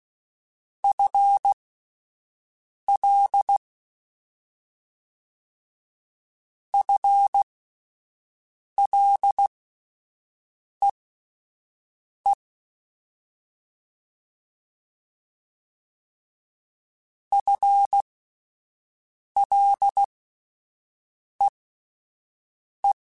Morse code, heard as a keyed tone, about 16 wpm, spaced as for 5 wpm, 800 Hz. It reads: FL FLEE FLEE
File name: Combinations / Combinations-mp3s / Combinations@5